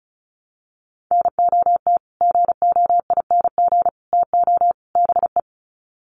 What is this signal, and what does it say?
Morse code, heard as a keyed tone, about 34 wpm, 700 Hz. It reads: NOT GOING TO BE
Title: Sets of 4 Words 35wpm